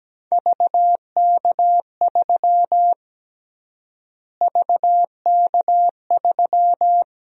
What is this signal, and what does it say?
Morse code, heard as a keyed tone, about 17 wpm, 700 Hz. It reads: VK3 VK3
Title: USA Call Sign Prefixes - Repeated 2x - 17wpm